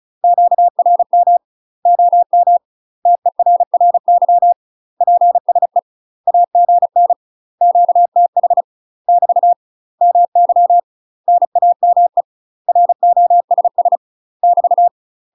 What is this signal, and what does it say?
Morse code, heard as a keyed tone, about 36 wpm, 700 Hz. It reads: QRM OM TERRY PSE AGN QTH <BT> MY NAME ROSS <BT>
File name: UK MASTER QSOs TRAINING / 35WPM / UK MASTER QSOs 0035